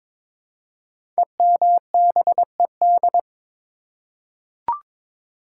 Morse code, heard as a keyed tone, about 22 wpm, 700 Hz. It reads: EMBED
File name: 5 Letter Words - First 3 Letters Given - 22wpm